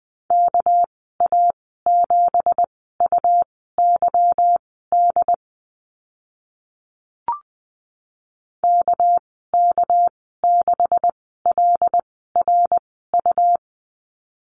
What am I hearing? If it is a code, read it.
Morse code, heard as a keyed tone, about 20 wpm, spaced as for 14 wpm, 700 Hz. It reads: KA7UYD KK6LRU
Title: 207 Lesson 33g - Letter j - Call Signs 20wpm